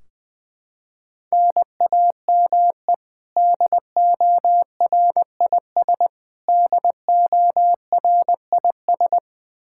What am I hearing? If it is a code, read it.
Morse code, heard as a keyed tone, about 21 wpm, 700 Hz. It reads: NAME DORIS DORIS
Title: Practice QSO - Exchange 1 - Beginning - 20wpm